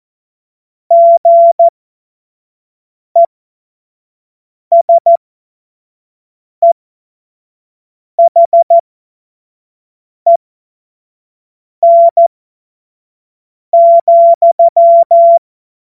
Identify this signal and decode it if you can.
Morse code, heard as a keyed tone, about 14 wpm, spaced as for 5 wpm, 680 Hz. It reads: GESEHEN,
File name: Morsekurs / Witze / Witz 08. Februar 2022 25 F70